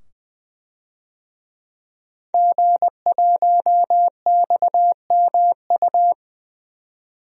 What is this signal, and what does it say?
Morse code, heard as a keyed tone, about 21 wpm, 700 Hz. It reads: G1XMU
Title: DXCC-100 Call Signs - 5 Characters - 20wpm